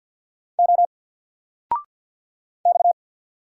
Morse code, heard as a keyed tone, about 48 wpm, 700 Hz. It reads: O X
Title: Single Character (Letter:Number:Punctuation:Prosign) Alt 50wpm